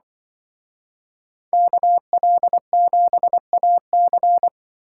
Morse code, heard as a keyed tone, about 25 wpm, 700 Hz. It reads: KL7AC
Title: Speed Racing - USA 2x2 Call Signs 20wpm